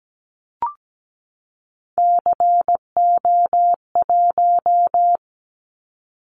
Morse code, heard as a keyed tone, about 17 wpm, 700 Hz. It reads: CO1
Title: Callsign Prefixes for Top 100 DXCC - 17wpm